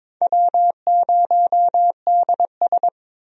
Morse code, heard as a keyed tone, about 22 wpm, 700 Hz. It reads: W0DS
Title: USA Call Signs - 4 Characters - 22wpm